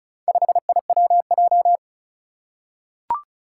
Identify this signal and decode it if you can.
Morse code, heard as a keyed tone, about 36 wpm, 700 Hz. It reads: 5IWJ
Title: Four Challenging Letter-Number Combinations 35wpm